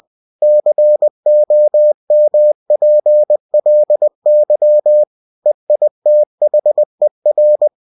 Morse code, heard as a keyed tone, about 20 wpm, 600 Hz. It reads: COMPLY EITHER
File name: winnie-20wpm